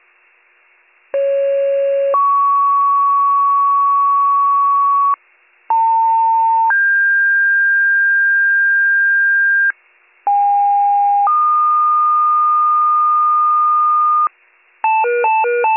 Two-tone page, heard as a one-second tone, then a three-second tone, about 550 then 1120 Hz, about 900 then 1600 Hz, about 810 then 1220 Hz. It is heard from 1.1 to 14.3 s.